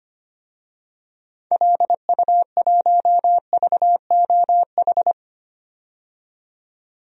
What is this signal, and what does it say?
Morse code, heard as a keyed tone, about 25 wpm, 700 Hz. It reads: LU1VOH
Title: DXCC-100 Call Signs - 6 Characters - 25wpm